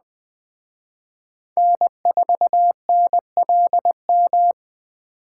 Morse code, heard as a keyed tone, about 24 wpm, 700 Hz. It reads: N4NLM
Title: Speed Racing - USA 1x3 Call Signs 20wpm